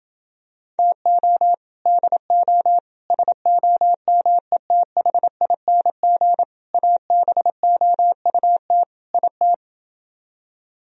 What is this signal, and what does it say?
Morse code, heard as a keyed tone, about 27 wpm, 700 Hz. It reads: TO DO SOMETHING ABOUT IT